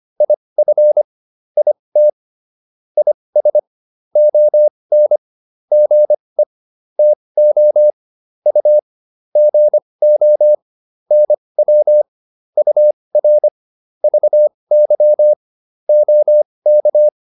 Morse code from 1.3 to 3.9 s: IT IS

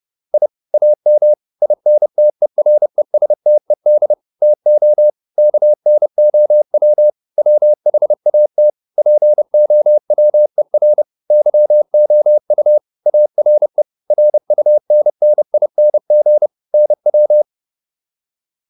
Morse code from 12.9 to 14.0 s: ARE

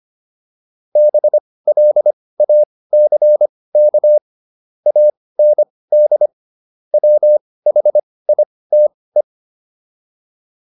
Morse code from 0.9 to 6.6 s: BLACK AND

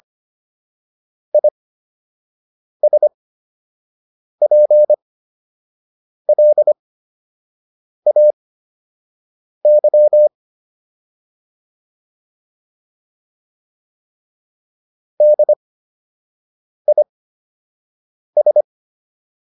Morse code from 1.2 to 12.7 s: ISPLAY

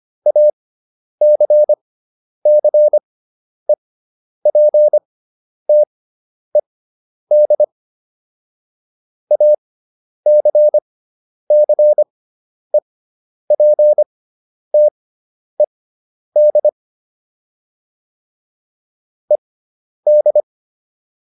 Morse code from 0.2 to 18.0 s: ACCEPTED ACCEPTED

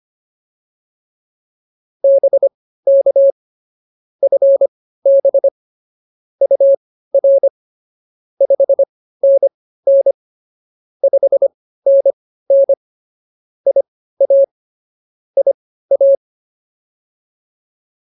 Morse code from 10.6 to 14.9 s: 5NN IA

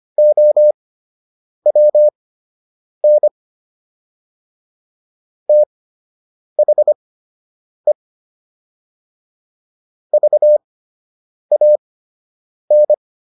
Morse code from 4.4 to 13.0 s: THE VAN

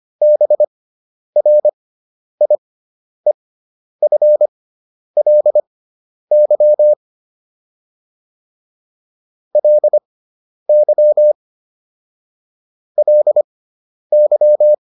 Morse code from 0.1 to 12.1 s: BRIEFLY LY

